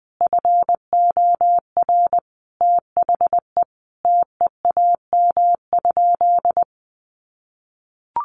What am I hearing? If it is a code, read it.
Morse code, heard as a keyed tone, about 20 wpm, 700 Hz. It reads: FOR THE TEAM?
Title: Sentences from Top 1500 Words 20wpm